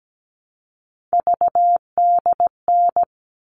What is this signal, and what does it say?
Morse code, heard as a keyed tone, about 17 wpm, 700 Hz. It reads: VDN
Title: All Cut Numbers - 3-Digit 17wpm